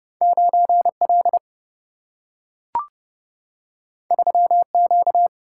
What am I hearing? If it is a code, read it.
Morse code, heard as a keyed tone, about 30 wpm, 700 Hz. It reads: 9L 3Q
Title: Single-Letter Single-Number 30wpm